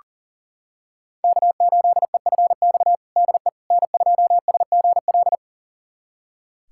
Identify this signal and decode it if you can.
Morse code, heard as a keyed tone, about 40 wpm, 700 Hz. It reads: K8EFX DE N2SGL
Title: USA Callsign DE Callsign - 5 Characters - 40wpm